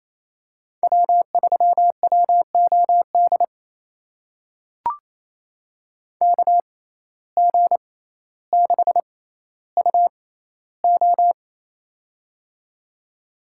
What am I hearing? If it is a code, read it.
Morse code, heard as a keyed tone, about 28 wpm, spaced as for 14 wpm, 700 Hz. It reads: W3WOD KG6UO
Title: USA Call Signs - 5 Characters - 6x Character Spacing - 28wpm